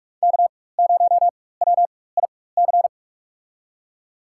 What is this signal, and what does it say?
Morse code, heard as a keyed tone, about 42 wpm, spaced as for 20 wpm, 700 Hz. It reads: K0WIC
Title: USA Call Signs - 5 Characters - 4x Character Spacing - 45wpm